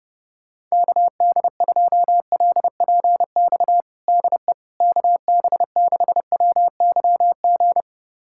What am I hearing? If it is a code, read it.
Morse code, heard as a keyed tone, about 30 wpm, 700 Hz. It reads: KD2LPX DE KB6WYG
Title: USA Callsign DE Callsign - 6 Characters - 30wpm